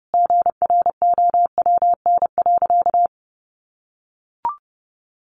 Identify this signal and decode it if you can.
Morse code, heard as a keyed tone, about 30 wpm, 700 Hz. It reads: GROWN.
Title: Easy Phrases 30wpm